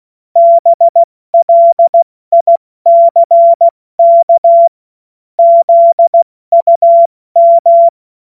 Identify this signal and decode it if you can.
Morse code, heard as a keyed tone, about 16 wpm, 680 Hz. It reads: BLICK ZUM
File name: Witz-02-September-2025-70-F80